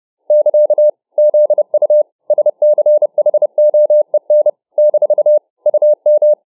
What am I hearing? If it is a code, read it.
Morse code, heard as a keyed tone, about 30 wpm, 600 Hz. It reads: <KA> ZU SCHOEN - UM